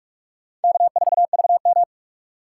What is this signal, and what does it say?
Morse code, heard as a keyed tone, about 57 wpm, 700 Hz. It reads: K4VM